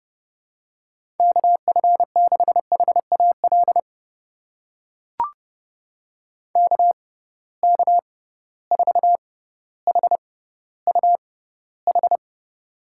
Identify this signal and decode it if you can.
Morse code, heard as a keyed tone, about 32 wpm, spaced as for 16 wpm, 700 Hz. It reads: KF6HAL KK4HUH
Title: USA Call Signs - 6x Character Spacing - 30wpm